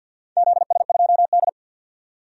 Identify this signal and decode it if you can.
Morse code, heard as a keyed tone, about 51 wpm, 700 Hz. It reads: ZS2D
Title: DXCC-100 Call Signs - 4 Characters - 50wpm